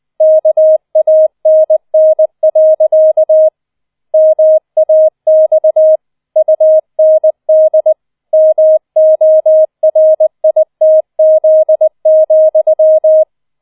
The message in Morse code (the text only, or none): KANN. MAX UND MORITZ,